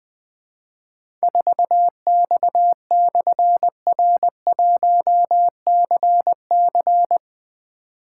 Morse code, heard as a keyed tone, about 20 wpm, 700 Hz. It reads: 4X/R1CC
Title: Difficult Call Signs 20wpm